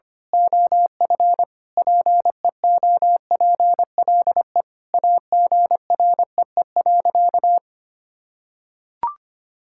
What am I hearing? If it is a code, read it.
Morse code, heard as a keyed tone, about 25 wpm, 700 Hz. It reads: OF PEOPLE AGREE.
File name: practice / 25wpm / Sentences with 4-Word Phrases - 25wpm